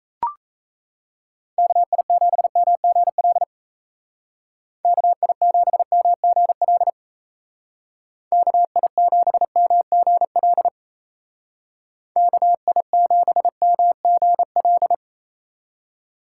Morse code, heard as a keyed tone, about 34 wpm, 700 Hz. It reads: KI7MGL KI7MGL KI7MGL KI7MGL
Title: Speed Racing - USA 2x3 Call Signs 28wpm